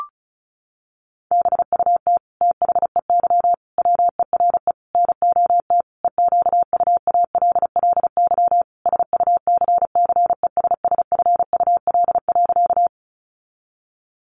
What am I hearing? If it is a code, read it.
Morse code, heard as a keyed tone, about 34 wpm, 700 Hz. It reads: BUT THEY WERE NOT EQUALLY SUCCESSFUL.
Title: Sentences from Top 1400 Words 35wpm